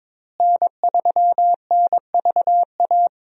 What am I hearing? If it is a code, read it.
Morse code, heard as a keyed tone, about 22 wpm, 700 Hz. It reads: N3NVA
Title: USA Call Signs - 5 Characters - Repeated 3x - 22wpm